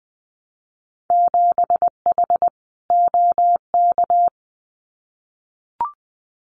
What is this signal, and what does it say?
Morse code, heard as a keyed tone, about 20 wpm, 700 Hz. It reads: 7H OK
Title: Winter Field Day Exchanges Encore 20wpm